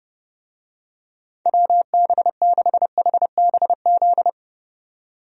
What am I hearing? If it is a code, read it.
Morse code, heard as a keyed tone, about 30 wpm, 700 Hz. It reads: WB6HBZ
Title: USA Call Signs - 6 Characters - 30wpm